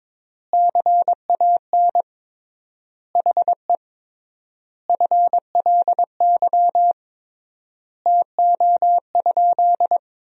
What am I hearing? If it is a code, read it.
Morse code, heard as a keyed tone, about 22 wpm, 700 Hz. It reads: CAN HE FLY TO?
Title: 3x Word-Spacing - Sentences from Top 500 Words - 22wpm